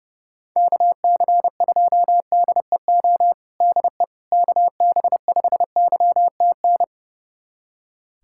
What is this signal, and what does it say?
Morse code, heard as a keyed tone, about 30 wpm, 700 Hz. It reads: KC2DEO DE KB5YTN
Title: USA Callsign DE Callsign 30wpm